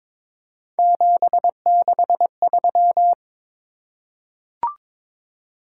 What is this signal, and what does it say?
Morse code, heard as a keyed tone, about 23 wpm, 700 Hz. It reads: 763
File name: practice / 22wpm / Three Letter-Number Combinations - 22wpm